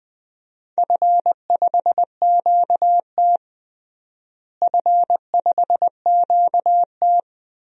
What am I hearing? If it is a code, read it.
Morse code, heard as a keyed tone, about 20 wpm, 700 Hz. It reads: F5QT F5QT
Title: DXCC-100 Call Signs - 4 Characters - Repeated 2x - 20wpm